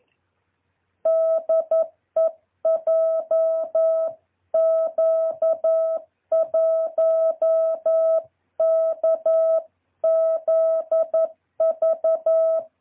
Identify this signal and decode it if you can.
Morse code, heard as a keyed tone, about 11 wpm, 650 Hz. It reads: DE JQ1KZV
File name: jh8cbh / 2023cw